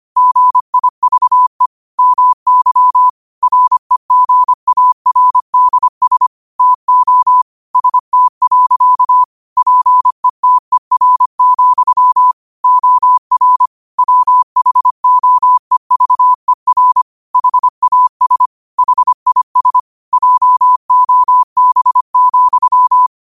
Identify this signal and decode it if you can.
Morse code, heard as a keyed tone, about 25 wpm, 1000 Hz. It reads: GIVE MY REGARDS TO ST. PETER, OR WHOEVER HAS HIS JOB,
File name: 25wpm